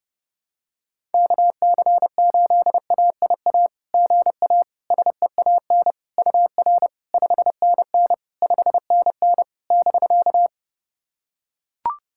Morse code, heard as a keyed tone, about 32 wpm, 690 Hz. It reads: KC8AIA GA SEAN UR 5NN 5NN <BK>
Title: SOTA QSO Practice 30wpm